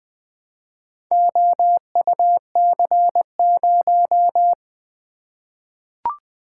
Morse code, heard as a keyed tone, about 20 wpm, 700 Hz. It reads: OUC0
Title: Four Letter-Number Combinations - 20wpm